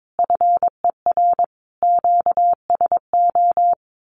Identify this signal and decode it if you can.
Morse code, heard as a keyed tone, about 22 wpm, 700 Hz. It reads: FER QSO
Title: Common QSO Phrases 22wpm